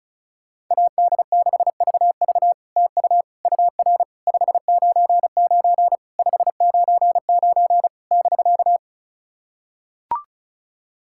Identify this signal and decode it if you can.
Morse code, heard as a keyed tone, about 35 wpm, 700 Hz. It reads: AD6VV TU UR 599 599 <BK>
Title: SOTA QSO Practice 35wpm